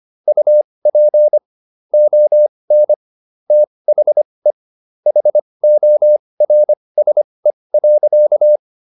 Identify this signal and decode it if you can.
Morse code, heard as a keyed tone, about 25 wpm, spaced as for 20 wpm, 600 Hz. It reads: UP ON THE HORSE.